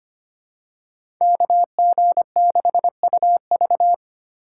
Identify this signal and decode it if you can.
Morse code, heard as a keyed tone, about 25 wpm, 700 Hz. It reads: KG6UV
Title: USA Call Signs - 5 Characters - 4x Character Spacing - 25wpm